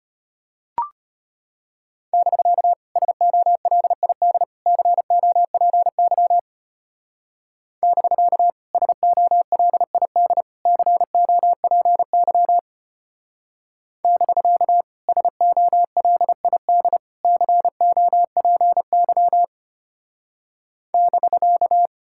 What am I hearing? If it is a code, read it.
Morse code, heard as a keyed tone, about 32 wpm, 700 Hz. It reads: <BK> SOLID COPY <BK> SOLID COPY <BK> SOLID COPY <BK>